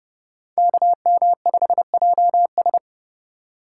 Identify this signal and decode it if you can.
Morse code, heard as a keyed tone, about 29 wpm, 700 Hz. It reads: KM5JS